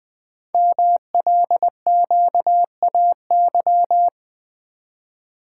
Morse code, heard as a keyed tone, about 20 wpm, 700 Hz. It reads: MLQAY
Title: Code Groups - 5 Letters - 20wpm